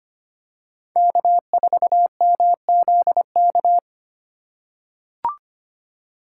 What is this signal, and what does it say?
Morse code, heard as a keyed tone, about 25 wpm, 700 Hz. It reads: K4MZK
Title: USA Call Signs - 5 Characters - 2x Character Spacing - 25wpm